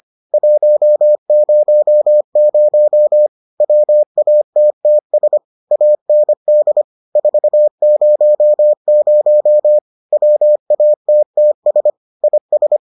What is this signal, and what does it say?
Morse code, heard as a keyed tone, about 25 wpm, 600 Hz. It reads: 100 WATTS AND 400 WATTS IS